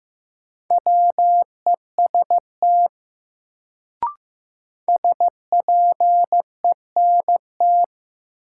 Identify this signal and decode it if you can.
Morse code, heard as a keyed tone, about 15 wpm, 700 Hz. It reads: WEST SPENT
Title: Sets of 5 Words - New Word Drill - 15wpm